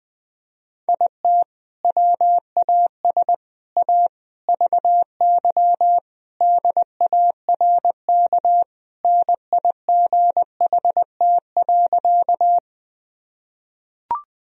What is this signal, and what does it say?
Morse code, heard as a keyed tone, about 21 wpm, 700 Hz. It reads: IT WAS A VY DARK NIGHT.
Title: Sentences from Top 500 Words and Abbreviations 20wpm